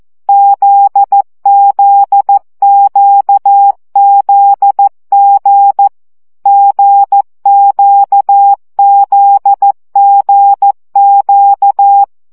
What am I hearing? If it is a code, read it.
Morse code, heard as a keyed tone, about 14 wpm, 800 Hz. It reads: ZZQZG GQZGQ
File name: CW gyakorlas g z q 12 baud